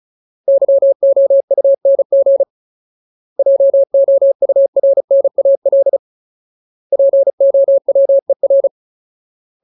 Morse code, heard as a keyed tone, about 34 wpm, 550 Hz. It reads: YOUNG JOURNAL POWER
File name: Ultimate Word List / 35WPM Ultimate